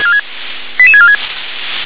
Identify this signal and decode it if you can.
Five-tone selective call (ZVEI), heard as quick runs of five tones, tones of about 70 ms. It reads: 88646